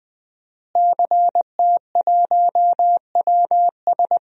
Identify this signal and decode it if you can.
Morse code, heard as a keyed tone, about 20 wpm, 700 Hz. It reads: CT1WS
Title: DXCC-100 Call Signs - 5 Characters - Repeated 2x - 20wpm